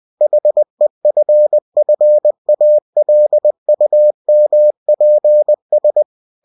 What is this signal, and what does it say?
Morse code, heard as a keyed tone, about 20 wpm, 600 Hz. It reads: HEFFALUMPS
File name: cw-books / winnie-the-pooh / winnie-20wpm-page-0043